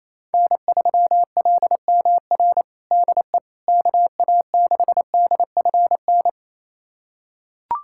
Morse code, heard as a keyed tone, about 28 wpm, 700 Hz. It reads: N3LMR DE KA6DFN
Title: USA Callsign DE Callsign 28wpm